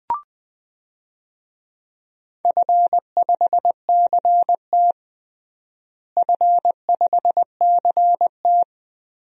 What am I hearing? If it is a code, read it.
Morse code, heard as a keyed tone, about 20 wpm, 700 Hz. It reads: F5CT F5CT
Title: DXCC-100 Call Signs - 4 Characters - Repeated 2x - 20wpm